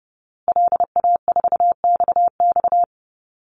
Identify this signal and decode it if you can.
Morse code, heard as a keyed tone, about 30 wpm, 700 Hz. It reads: LA4XX
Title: Call Signs Repeated 2x - 30wpm